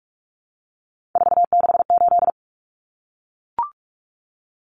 Morse code, heard as a keyed tone, about 44 wpm, 700 Hz. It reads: 468
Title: Numbers - 3-Digit 45wpm